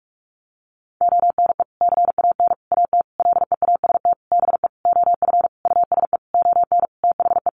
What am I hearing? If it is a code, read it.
Morse code, heard as a keyed tone, about 44 wpm, 700 Hz. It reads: ONE CAN AT LEAST BE OF USE ON THE
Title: Sentences from Top 1200 Words 45wpm